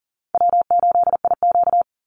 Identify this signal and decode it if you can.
Morse code, heard as a keyed tone, about 40 wpm, 700 Hz. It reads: W8IQ
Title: Easy Call Signs - 40wpm